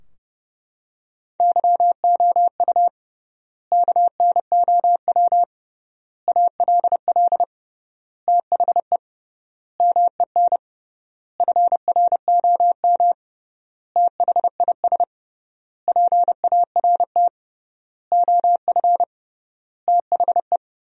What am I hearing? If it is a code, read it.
Morse code, heard as a keyed tone, about 30 wpm, 700 Hz. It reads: YOU KNOW ALL THE MEN FROM THIS PART OF THE
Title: 3x Word-Spacing - Sentences from Top 300 Words - 30wpm